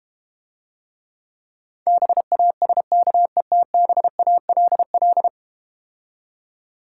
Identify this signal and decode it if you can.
Morse code, heard as a keyed tone, about 32 wpm, 690 Hz. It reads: BASKETBALL